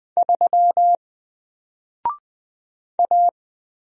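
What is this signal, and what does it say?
Morse code, heard as a keyed tone, about 20 wpm, 700 Hz. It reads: 3 A
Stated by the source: Single letter and number Morse drill at 20 wpm, from N4LCD.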